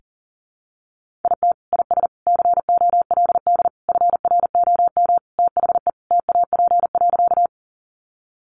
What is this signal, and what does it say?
Morse code, heard as a keyed tone, about 37 wpm, 700 Hz. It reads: IT IS COLD FROM THE TAP.
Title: Sentences from Top 1400 Words 40wpm